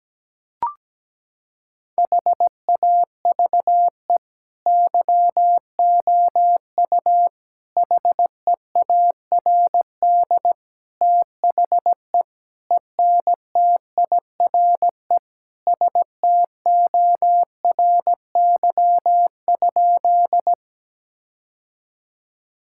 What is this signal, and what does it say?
Morse code, heard as a keyed tone, about 17 wpm, 700 Hz. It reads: HAVE YOU HEARD THE ENTIRE STORY?
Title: Sentences from Top 1000 Words - Review Part 5 - 17wpm